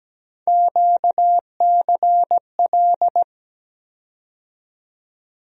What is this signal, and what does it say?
Morse code, heard as a keyed tone, about 17 wpm, 700 Hz. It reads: QCL
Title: USA Call Sign Postfixes - 3 Letters 17wpm